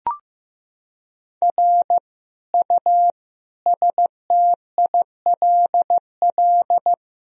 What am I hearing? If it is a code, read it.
Morse code, heard as a keyed tone, about 16 wpm, 700 Hz. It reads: R U STILL
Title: Sentences from Top 400 Words and Abbreviations 15wpm